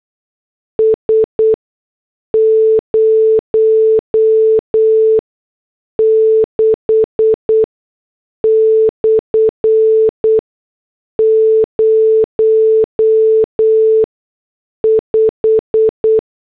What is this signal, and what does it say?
Morse code, heard as a keyed tone, about 8 wpm, spaced as for 6 wpm, 440 Hz. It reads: S 06/05